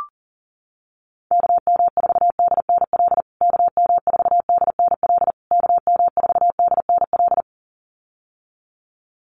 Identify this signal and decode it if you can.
Morse code, heard as a keyed tone, about 37 wpm, 700 Hz. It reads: KM4DNL KM4DNL KM4DNL
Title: USA Call Signs Repeated 3x 40wpm